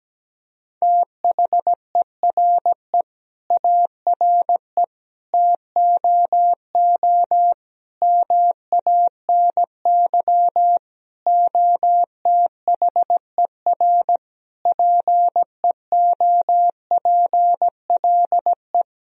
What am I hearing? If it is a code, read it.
Morse code, heard as a keyed tone, about 17 wpm, 700 Hz. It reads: THERE ARE TOO MANY OTHER PEOPLE
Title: Sentences from Top 1000 Words - Review Part 3 - 17wpm